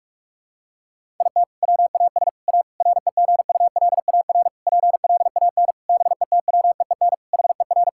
Morse code, heard as a keyed tone, about 46 wpm, 700 Hz. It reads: IT WAS A REGULAR PLAN BETWEEN HER
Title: Sentences from Top 1000 Words - Review Part 6 - 45wpm